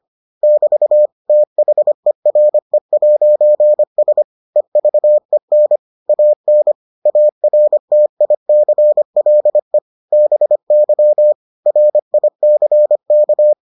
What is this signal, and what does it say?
Morse code, heard as a keyed tone, about 25 wpm, 600 Hz. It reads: <BT> THERE'S EVEN AN ARTICLE BY RICK